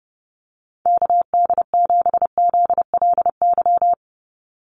Morse code, heard as a keyed tone, about 30 wpm, 700 Hz. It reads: KD7ZLY
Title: USA Call Signs 30wpm